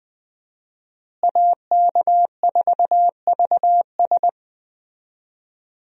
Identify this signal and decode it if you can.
Morse code, heard as a keyed tone, about 21 wpm, 700 Hz. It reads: AK4VS
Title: USA Call Signs - 5 Characters - Repeated 3x - 20wpm